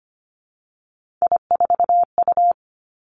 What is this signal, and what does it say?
Morse code, heard as a keyed tone, about 25 wpm, 700 Hz. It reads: I4U
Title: Three Challenging Letter-Number Combinations 25wpm